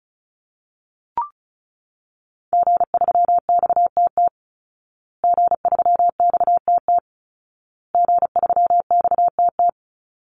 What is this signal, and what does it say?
Morse code, heard as a keyed tone, about 34 wpm, 700 Hz. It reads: G3XTT G3XTT G3XTT
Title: Call Signs Repeated 2x - 35wpm